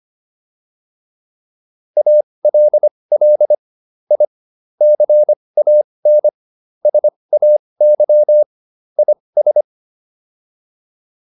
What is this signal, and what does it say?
Morse code, heard as a keyed tone, about 25 wpm, spaced as for 20 wpm, 600 Hz. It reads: ALL I CAN SAY IS